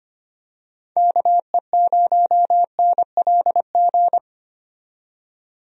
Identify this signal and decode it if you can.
Morse code, heard as a keyed tone, about 25 wpm, 700 Hz. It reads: KE0NLG
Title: USA Call Signs - 6 Characters - 25wpm